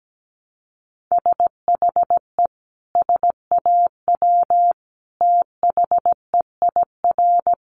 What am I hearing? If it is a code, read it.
Morse code, heard as a keyed tone, about 17 wpm, 700 Hz. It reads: SHE SAW THEIR
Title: Sentences from Top 1200 Words 17wpm